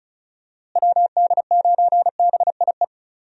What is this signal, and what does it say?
Morse code, heard as a keyed tone, about 34 wpm, 700 Hz. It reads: WD9BIE
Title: USA Call Signs - 6 Characters - Repeated 3x - 35wpm